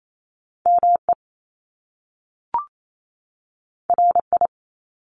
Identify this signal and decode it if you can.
Morse code, heard as a keyed tone, about 28 wpm, 700 Hz. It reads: ME RI
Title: Top Two Letter Combinations in Words 28wpm